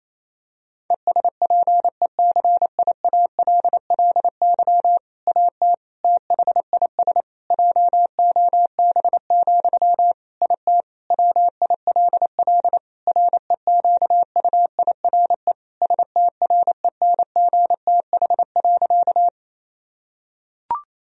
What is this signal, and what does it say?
Morse code, heard as a keyed tone, about 27 wpm, 700 Hz. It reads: ESPECIALLY AT THIS JOB, IT WILL REQUIRE STRENGTH.
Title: Sentences from Top 1000 Words - Review Part 4 - 28wpm